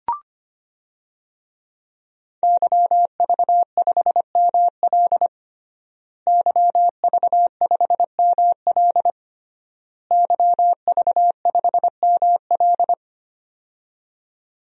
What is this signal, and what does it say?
Morse code, heard as a keyed tone, about 25 wpm, 700 Hz. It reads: YV5ML YV5ML YV5ML
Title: DXCC-100 Call Signs - 5 Characters - Repeated 3x - 25wpm